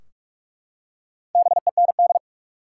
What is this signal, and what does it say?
Morse code, heard as a keyed tone, about 45 wpm, 700 Hz. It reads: BEND
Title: Four Letter Words - 45wpm